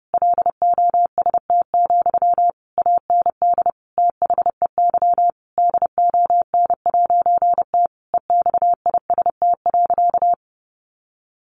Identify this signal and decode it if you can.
Morse code, heard as a keyed tone, about 30 wpm, 700 Hz. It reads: LOST, AND THEY DON'T EXIST.
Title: A Sound Of Thunder 30wpm